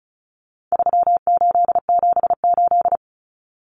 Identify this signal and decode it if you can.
Morse code, heard as a keyed tone, about 34 wpm, 700 Hz. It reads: 3878
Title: Numbers - 4-Digit 35wpm